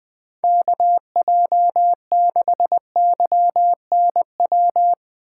Morse code, heard as a keyed tone, about 19 wpm, 700 Hz. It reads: KJ6YNW
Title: Speed Racing - USA Call Signs 17wpm